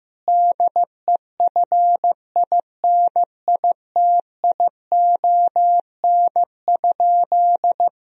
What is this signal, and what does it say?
Morse code, heard as a keyed tone, about 15 wpm, 700 Hz. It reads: DEFINITION?
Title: Sentences from Top 1000 Words - Review Part 4 - 15wpm